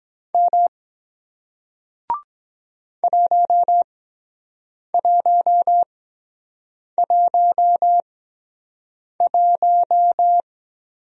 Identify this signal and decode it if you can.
Morse code, heard as a keyed tone, about 23 wpm, spaced as for 7 wpm, 700 Hz. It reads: M 1111